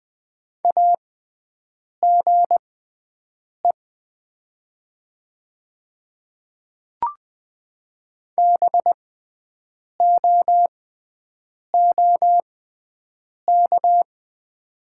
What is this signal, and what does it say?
Morse code, heard as a keyed tone, about 20 wpm, spaced as for 7 wpm, 700 Hz. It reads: AGE BOOK
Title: 6x Character Spacing - Top 500 Words - 20wpm